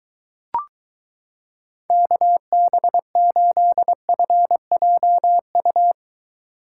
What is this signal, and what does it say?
Morse code, heard as a keyed tone, about 23 wpm, 700 Hz. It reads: KB8FJU